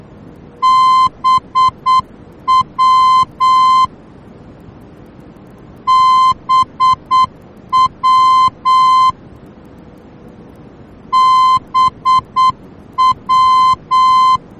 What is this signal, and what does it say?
Morse code, heard as a keyed tone, about 8 wpm, 1040 Hz. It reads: BW BW BW
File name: germany / mx / BW-D